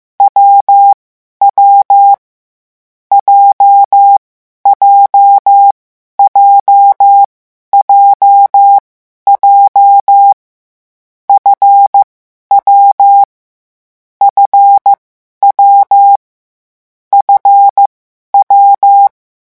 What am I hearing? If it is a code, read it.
Morse code, heard as a keyed tone, about 15 wpm, spaced as for 11 wpm, 800 Hz. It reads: W W J JJ JJ F W FW F W
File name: morse mp3 / lecon7-vitesse 14